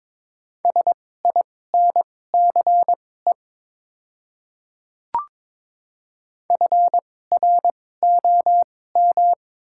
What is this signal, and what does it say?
Morse code, heard as a keyed tone, about 22 wpm, spaced as for 16 wpm, 700 Hz. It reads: SINCE FROM
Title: 2x Character Spacing - Top 500 Words - 22wpm